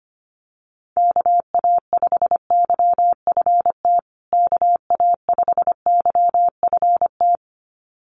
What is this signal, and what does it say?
Morse code, heard as a keyed tone, about 25 wpm, 700 Hz. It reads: KA5YFT KA5YFT
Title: USA Call Signs Repeated 2x 25wpm